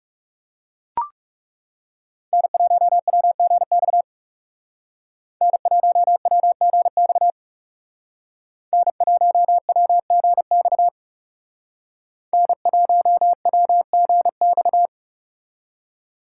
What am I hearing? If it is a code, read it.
Morse code, heard as a keyed tone, about 36 wpm, 700 Hz. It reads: N1WGX N1WGX N1WGX N1WGX
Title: Speed Racing - USA 1x3 Call Signs 30wpm